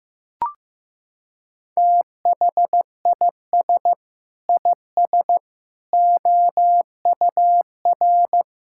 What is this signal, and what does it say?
Morse code, heard as a keyed tone, about 15 wpm, 700 Hz. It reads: THIS IS OUR
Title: Sentences from Top 1000 Words - Review Part 5 - 15wpm